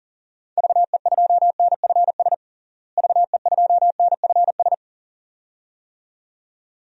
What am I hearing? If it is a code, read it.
Morse code, heard as a keyed tone, about 40 wpm, 700 Hz. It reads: VE2NFS VE2NFS
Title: DXCC-100 Call Signs - 6 Characters - Repeated 2x - 40wpm